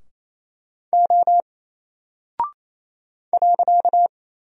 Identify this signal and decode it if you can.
Morse code, heard as a keyed tone, about 28 wpm, 700 Hz. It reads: O .
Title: Single Character (Letter:Number:Punctuation:Prosign) Alt 28wpm